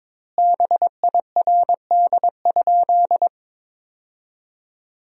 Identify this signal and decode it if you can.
Morse code, heard as a keyed tone, about 22 wpm, 700 Hz. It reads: BIRD?
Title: 2x Word-Spacing - Sentences from Top 400 Words - 22wpm